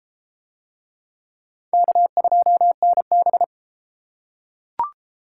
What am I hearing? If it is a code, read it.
Morse code, heard as a keyed tone, about 30 wpm, 700 Hz. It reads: K2NB